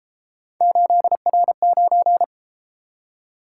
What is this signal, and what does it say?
Morse code, heard as a keyed tone, about 32 wpm, 700 Hz. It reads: 8R9